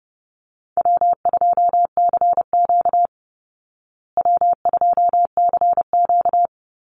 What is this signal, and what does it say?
Morse code, heard as a keyed tone, about 30 wpm, 700 Hz. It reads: W2CQ W2CQ
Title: Easy Call Signs Repeated 3x - 30wpm